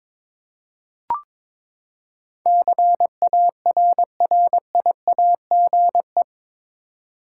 Morse code, heard as a keyed tone, about 21 wpm, 700 Hz. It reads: CARRIAGE
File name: practice / 22wpm / Eight Letter Words - 22wpm